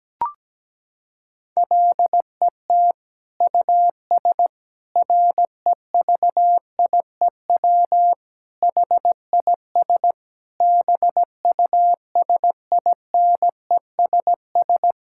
Morse code, heard as a keyed tone, about 17 wpm, 700 Hz. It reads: LET US REVIEW HIS BUSINESS
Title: Sentences from Top 1000 Words - Review Part 6 - 17wpm